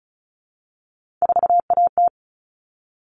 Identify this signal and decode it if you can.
Morse code, heard as a keyed tone, about 34 wpm, 700 Hz. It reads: 4AT
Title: All Cut Numbers - 3-Digit 35wpm